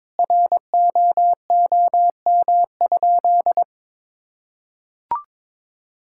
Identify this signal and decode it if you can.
Morse code, heard as a keyed tone, about 22 wpm, 700 Hz. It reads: ROOM?
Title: Sentences from Top 1000 Words - Review Part 5 - 22wpm